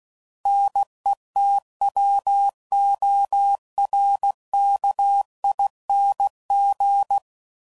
Morse code, heard as a keyed tone, about 16 wpm, 800 Hz. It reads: NETWORKING
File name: Suffixes / Special INGs / Sngl Ings